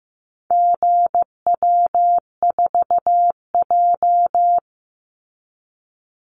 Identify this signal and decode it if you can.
Morse code, heard as a keyed tone, about 15 wpm, 700 Hz. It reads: GW4J
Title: Call Signs Repeated 2x - 15wpm